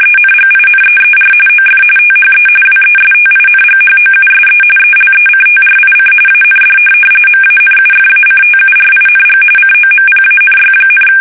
RTTY, about 45.45 baud. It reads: ENGINEER OF THE WESTINGHOUSE ELECTRIC AND MANUFACTURING COMPANY,
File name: fsk850 / benjamin-g-lamme